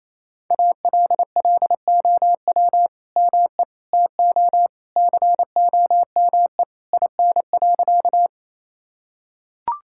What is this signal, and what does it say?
Morse code, heard as a keyed tone, about 28 wpm, 700 Hz. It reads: ALLOW ME TO COME IN.
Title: Sentences from Top 1000 Words - Review Part 5 - 28wpm